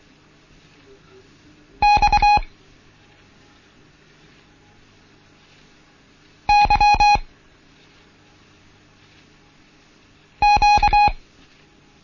Morse code, heard as a keyed tone, about 24 wpm, spaced as for 3 wpm, 830 Hz. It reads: X YQ
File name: Morsen / Kurs A 16